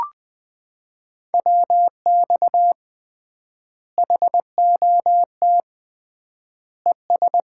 Morse code, heard as a keyed tone, about 21 wpm, 700 Hz. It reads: WX HOT ES
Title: Common QSO Phrases - 3x Word-Spacing - 20wpm